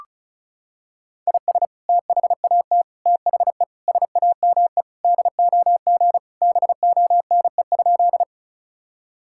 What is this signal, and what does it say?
Morse code, heard as a keyed tone, about 36 wpm, 700 Hz. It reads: IS THAT THE SAME DOG BONE?
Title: Sentences from Top 1600 Words 35wpm